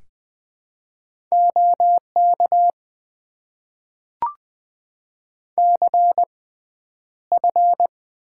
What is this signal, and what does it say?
Morse code, heard as a keyed tone, about 21 wpm, spaced as for 10 wpm, 700 Hz. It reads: OK CF
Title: Top 90 QSO Elements - 6x Character Spacing - 20wpm